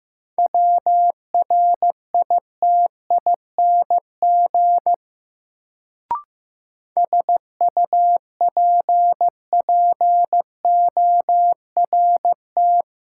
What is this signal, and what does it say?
Morse code, heard as a keyed tone, about 15 wpm, 700 Hz. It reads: WRITING SUPPORT
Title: Sentences from Top 800 Words - New Word Drill - 15wpm